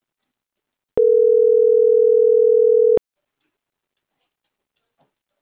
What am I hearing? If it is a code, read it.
Ringback tone (North America), 440 + 480 Hz.
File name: CENTRAL MAERSK / out polanco.msk - .43246